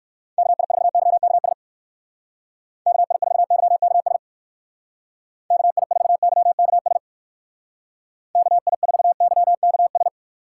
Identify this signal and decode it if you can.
Morse code, heard as a keyed tone, about 57 wpm, 700 Hz. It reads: KI4YKS KI4YKS KI4YKS KI4YKS